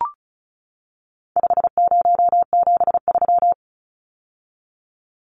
Morse code, heard as a keyed tone, about 34 wpm, 700 Hz. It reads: 5073
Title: Numbers - 4-Digit 35wpm